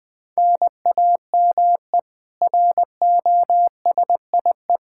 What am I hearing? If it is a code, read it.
Morse code, heard as a keyed tone, about 21 wpm, 700 Hz. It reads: NAME ROSIE
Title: K1USN SST QSO Practice 20wpm